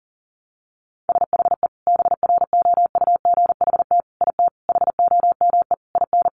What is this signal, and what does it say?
Morse code, heard as a keyed tone, about 40 wpm, 700 Hz. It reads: SHE BROUGHT IT HOME IN
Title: Sentences from Top 500 Words 40wpm